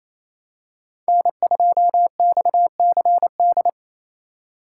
Morse code, heard as a keyed tone, about 28 wpm, 700 Hz. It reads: N2XCD
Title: USA Call Signs - 6x Character Spacing - 28wpm